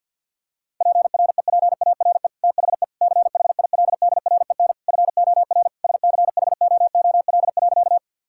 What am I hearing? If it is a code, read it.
Morse code, heard as a keyed tone, about 47 wpm, 700 Hz. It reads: PREPARE THE CHILDREN FOR SCHOOL.
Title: Sentences from Top 1000 Words - Review Part 6 - 50wpm